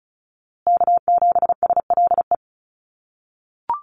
Morse code, heard as a keyed tone, about 34 wpm, 700 Hz. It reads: K7SLE
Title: USA Call Signs Repeated 3x 35wpm